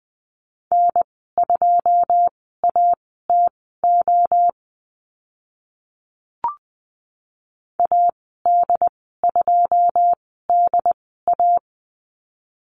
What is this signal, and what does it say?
Morse code, heard as a keyed tone, about 20 wpm, spaced as for 14 wpm, 700 Hz. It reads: N2ATO AD2DA
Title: 082 Lesson 14g - Number 2 - Call Signs 20wpm